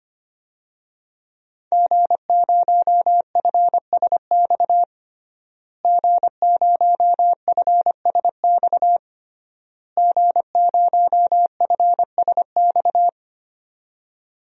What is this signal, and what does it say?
Morse code, heard as a keyed tone, about 25 wpm, 700 Hz. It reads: G0FSX G0FSX G0FSX
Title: DXCC-100 Call Signs - 5 Characters - Repeated 3x - 25wpm